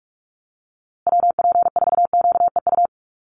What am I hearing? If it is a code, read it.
Morse code, heard as a keyed tone, about 44 wpm, 700 Hz. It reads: WP4QEU
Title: USA Call Signs Repeated 3x 45wpm